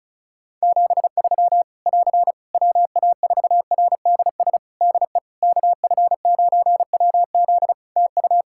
Morse code, heard as a keyed tone, about 36 wpm, 700 Hz. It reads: 73 <AR> WA4RDS DE KF9WZ TU
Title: K1USN SST QSO Practice 35wpm